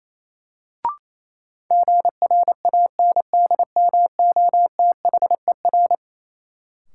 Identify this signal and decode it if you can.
Morse code, heard as a keyed tone, about 28 wpm, 700 Hz. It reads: GRANDMOTHER
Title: Sentences from Top 1200 Words - New Word Drill Alt - 28wpm